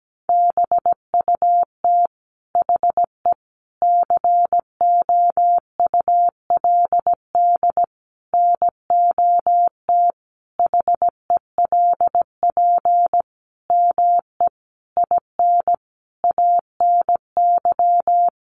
Morse code, heard as a keyed tone, about 17 wpm, 700 Hz. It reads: BUT HE COULD NOT HELP ME IN ANY
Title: Sentences from Top 600 Words 17wpm